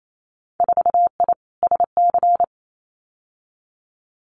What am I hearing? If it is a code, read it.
Morse code, heard as a keyed tone, about 28 wpm, 700 Hz. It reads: 4I SC
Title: Winter Field Day Exchanges 28wpm